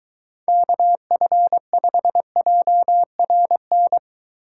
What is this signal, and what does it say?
Morse code, heard as a keyed tone, about 23 wpm, 700 Hz. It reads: KF5JRN